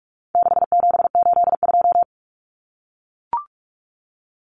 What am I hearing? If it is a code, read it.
Morse code, heard as a keyed tone, about 44 wpm, 700 Hz. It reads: 6782
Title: Common Cut Numbers - 4-Digit 45wpm